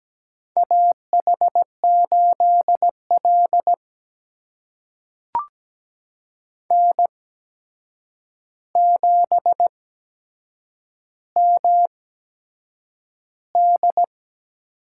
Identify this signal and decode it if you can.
Morse code, heard as a keyed tone, about 17 wpm, spaced as for 7 wpm, 700 Hz. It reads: AH8L N7MD
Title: USA Call Signs - 4 Characters - 8x Character Spacing - 17wpm